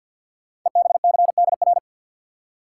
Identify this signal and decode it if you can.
Morse code, heard as a keyed tone, about 52 wpm, 700 Hz. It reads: EBCDR
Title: Code Groups - 5 Letters - 50wpm